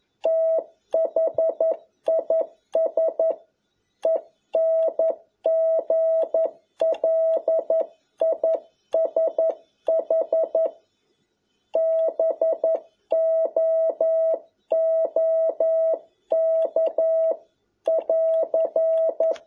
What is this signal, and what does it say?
Morse code, heard as a keyed tone, about 11 wpm, 650 Hz. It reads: THIS ENGLISH BOOK <AR>
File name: jh8cbh / 2021cw